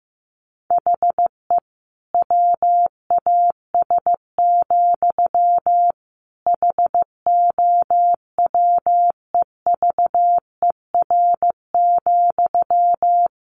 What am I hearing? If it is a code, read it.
Morse code, heard as a keyed tone, about 15 wpm, 700 Hz. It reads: HE WAS, HOWEVER,
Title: Sentences from Top 1100 Words 15wpm